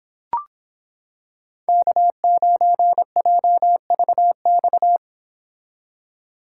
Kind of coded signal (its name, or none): Morse code